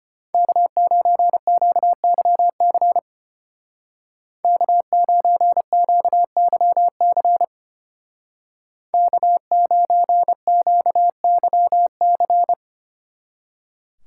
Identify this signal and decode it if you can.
Morse code, heard as a keyed tone, about 30 wpm, 700 Hz. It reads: K9QYC K9QYC K9QYC
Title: Speed Racing - USA 1x3 Call Signs 25wpm